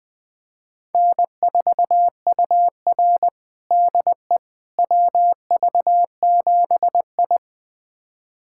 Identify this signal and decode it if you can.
Morse code, heard as a keyed tone, about 20 wpm, 700 Hz. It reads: N4UR DE WV7I
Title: USA Callsign DE Callsign - 4 Characters - 20wpm